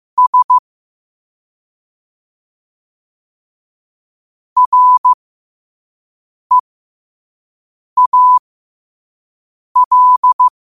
Morse code, heard as a keyed tone, about 15 wpm, spaced as for 5 wpm, 1000 Hz. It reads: S REAL
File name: qotdfiles / 5wpm